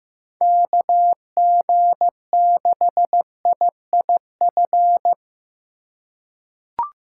Morse code, heard as a keyed tone, about 15 wpm, 700 Hz. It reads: KG6IIF
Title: USA Call Signs - 6 Characters - 2x Character Spacing - 15wpm